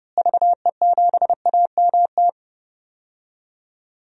Morse code, heard as a keyed tone, about 30 wpm, 700 Hz. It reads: VE7AMT
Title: DXCC-100 Call Signs - 6 Characters - Repeated 2x - 30wpm